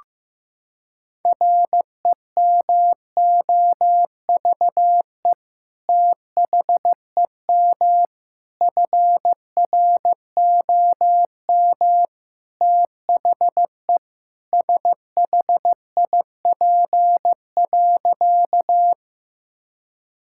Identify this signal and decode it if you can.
Morse code, heard as a keyed tone, about 15 wpm, 700 Hz. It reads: REMOVE THEM FROM THE SHIP.
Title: Sentences from Top 1000 Words - Review Part 5 - 15wpm